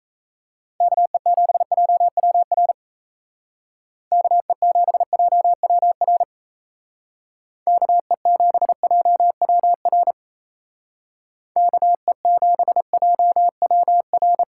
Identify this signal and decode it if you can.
Morse code, heard as a keyed tone, about 36 wpm, 700 Hz. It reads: KE7JWR KE7JWR KE7JWR KE7JWR